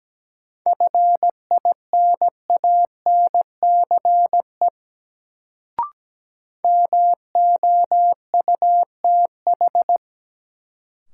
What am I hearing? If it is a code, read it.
Morse code, heard as a keyed tone, about 17 wpm, 700 Hz. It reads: FINANCE MOUTH
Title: Sentences from Top 1500 Words - New Word Drill - 17wpm